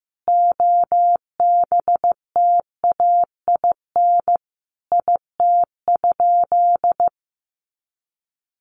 Morse code, heard as a keyed tone, about 14 wpm, 700 Hz. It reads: OBTAIN IT?
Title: Sentences from Top 1300 Words 15wpm